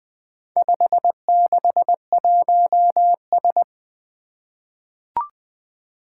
Morse code, heard as a keyed tone, about 20 wpm, 700 Hz. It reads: 561S
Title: Four Challenging Letter-Number Combinations 20wpm